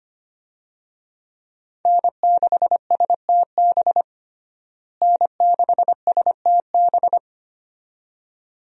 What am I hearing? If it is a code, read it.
Morse code, heard as a keyed tone, about 25 wpm, 700 Hz. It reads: N6STB N6STB
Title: USA Call Signs - 5 Characters - Repeated 2x - 25wpm